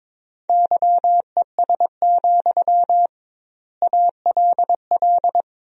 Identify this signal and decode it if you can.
Morse code, heard as a keyed tone, about 22 wpm, 700 Hz. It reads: YES, ALL
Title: 2x Word-Spacing - Sentences from Top 500 Words - 22wpm